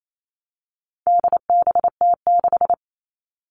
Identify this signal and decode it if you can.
Morse code, heard as a keyed tone, about 28 wpm, 700 Hz. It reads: DBT6
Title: All Cut Numbers - 4-Digit 28wpm